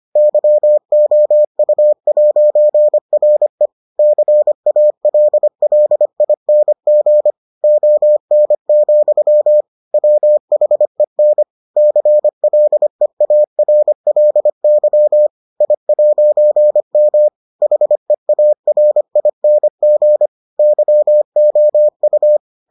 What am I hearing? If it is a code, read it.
Morse code, heard as a keyed tone, about 25 wpm, 600 Hz. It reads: YOU'RE CALLING ON, WHEN CLEARLY I'M HEARING YOU